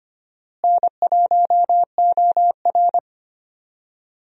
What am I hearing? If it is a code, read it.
Morse code, heard as a keyed tone, about 25 wpm, 700 Hz. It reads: N1OR
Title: USA Call Signs - 4 Characters - Repeated 2x - 25wpm